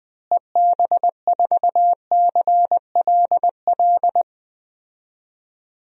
Morse code, heard as a keyed tone, about 20 wpm, 700 Hz. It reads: EB4CLL
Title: DXCC-100 Call Signs - 6 Characters - Repeated 3x - 20wpm